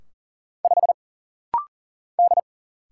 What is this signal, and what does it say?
Morse code, heard as a keyed tone, about 40 wpm, 700 Hz. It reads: 5 D
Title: Single Letter-Number - Rapid-Fire 40wpm